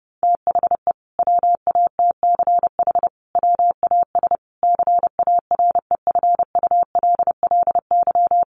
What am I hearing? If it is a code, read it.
Morse code, heard as a keyed tone, about 30 wpm, 700 Hz. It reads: THE WATCH WAS CAREFULLY
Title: Sentences from Top 1300 Words 30wpm